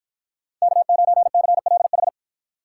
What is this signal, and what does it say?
Morse code, heard as a keyed tone, about 53 wpm, 700 Hz. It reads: K9CLH